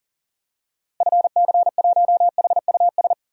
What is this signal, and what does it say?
Morse code, heard as a keyed tone, about 40 wpm, 700 Hz. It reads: FC1HUS
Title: DXCC-100 Call Signs - 6 Characters - 40wpm